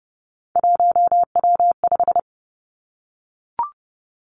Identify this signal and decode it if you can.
Morse code, heard as a keyed tone, about 31 wpm, 700 Hz. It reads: 1W5
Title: Three Challenging Letter-Number Combinations 30wpm